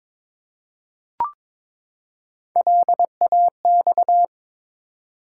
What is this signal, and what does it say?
Morse code, heard as a keyed tone, about 22 wpm, 700 Hz. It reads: LAX
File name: ARRL Sections v2 - 22wpm